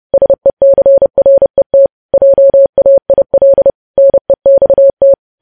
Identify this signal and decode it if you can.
Morse code, heard as a keyed tone, about 30 wpm, 550 Hz. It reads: SECRET JAIL NEXT
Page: news headlines in morse code at 30 wpm